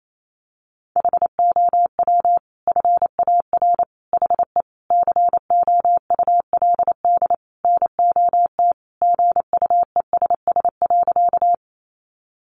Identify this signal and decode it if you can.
Morse code, heard as a keyed tone, about 28 wpm, 700 Hz. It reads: HOW FAR HE COULD NOT GUESS.
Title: Sentences from Top 1500 Words 28wpm